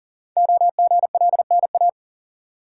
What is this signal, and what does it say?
Morse code, heard as a keyed tone, about 42 wpm, 690 Hz. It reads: OGLNA
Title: Code Groups - 5 Letters - 40wpm